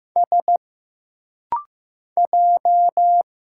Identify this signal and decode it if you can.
Morse code, heard as a keyed tone, about 15 wpm, 700 Hz. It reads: S J
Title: Single Letters - Rapid-Fire 15wpm